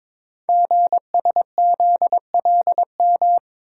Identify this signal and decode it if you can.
Morse code, heard as a keyed tone, about 22 wpm, 700 Hz. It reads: GSZLM
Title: Code Groups - 5 Letters - 22wpm